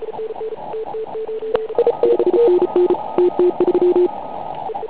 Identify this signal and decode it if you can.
Morse code, heard as a keyed tone, about 34 wpm, 360 Hz, with radio noise. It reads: 5NN TT3
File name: Pile UP CUC204